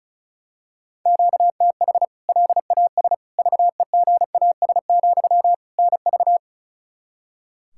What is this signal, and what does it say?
Morse code, heard as a keyed tone, about 36 wpm, 700 Hz. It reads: QTH LAS VEGAS, NV
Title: K1USN SST QSO Practice 35wpm